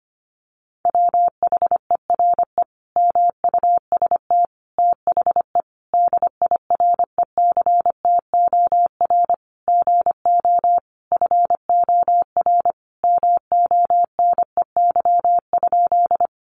Morse code, heard as a keyed tone, about 25 wpm, 700 Hz. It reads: WHERE MUST THE DIRECTOR GO FOR MONEY?
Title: Sentences from Top 1300 Words 25wpm